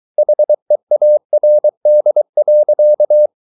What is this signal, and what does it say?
Morse code, heard as a keyed tone, about 23 wpm, 600 Hz. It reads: HEARD.